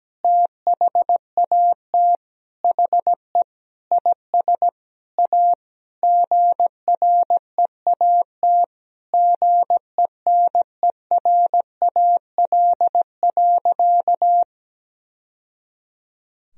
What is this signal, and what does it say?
Morse code, heard as a keyed tone, about 17 wpm, 700 Hz. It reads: THAT HE IS A GREAT GENERAL.
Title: Sentences from Top 1000 Words - Review Part 3 - 17wpm